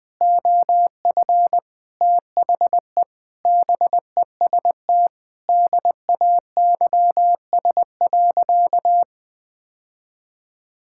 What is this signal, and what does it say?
Morse code, heard as a keyed tone, about 20 wpm, 700 Hz. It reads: OF THE BEST DAYS.
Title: Sentences with 4-Word Phrases - 20wpm